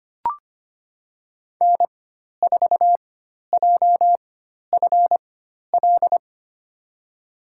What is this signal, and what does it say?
Morse code, heard as a keyed tone, about 25 wpm, spaced as for 12 wpm, 700 Hz. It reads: N4JFL
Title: USA Call Signs - 5 Characters - 4x Character Spacing - 25wpm